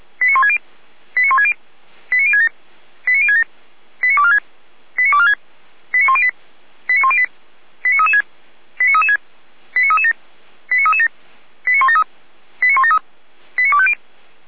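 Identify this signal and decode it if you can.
Five-tone selective call (ZVEI), heard as quick runs of five tones, tones of about 80 ms. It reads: 89160 89160 89067 89067 89246 89246 89108 89108 89336 89336 89337 89337 89173 89173 89250
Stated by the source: ZVEI five-tone call received by SDR.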